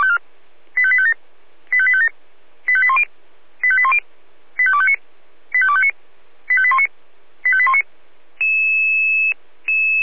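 Five-tone selective call (ZVEI), heard as quick runs of five tones, tones of about 70 ms. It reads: 85857 85857 85810 85810 85259 85259 85819 85819
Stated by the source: Five-tone ZVEI selective call, address 85814, received on an SDR.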